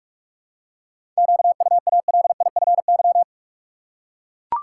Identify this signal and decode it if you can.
Morse code, heard as a keyed tone, about 43 wpm, 700 Hz. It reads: QUALIFY
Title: Seven Letter Words - 45wpm